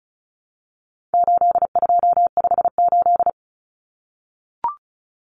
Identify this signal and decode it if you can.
Morse code, heard as a keyed tone, about 34 wpm, 700 Hz. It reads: 8258
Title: Common Cut Numbers - 4-Digit 35wpm